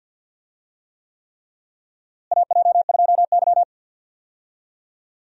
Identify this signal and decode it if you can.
Morse code, heard as a keyed tone, about 47 wpm, 700 Hz. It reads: AJ2Y
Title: USA Call Signs - 4 Characters - Repeated 3x - 50wpm